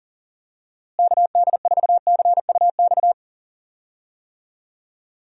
Morse code, heard as a keyed tone, about 37 wpm, 700 Hz. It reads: KD4CUX
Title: USA Call Signs - 6 Characters - Repeated 2x - 40wpm